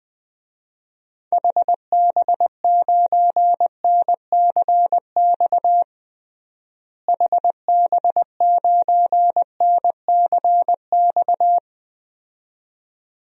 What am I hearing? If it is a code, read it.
Morse code, heard as a keyed tone, about 20 wpm, 700 Hz. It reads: HB9NCX HB9NCX
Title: DXCC-100 Call Signs - 6 Characters - Repeated 2x - 20wpm